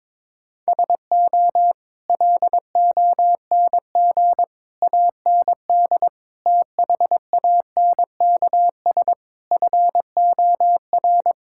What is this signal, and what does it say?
Morse code, heard as a keyed tone, about 21 wpm, 700 Hz. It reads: SO LONG AND THANKS FOR
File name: practice / 22wpm / Sets of 8 Words - 22wpm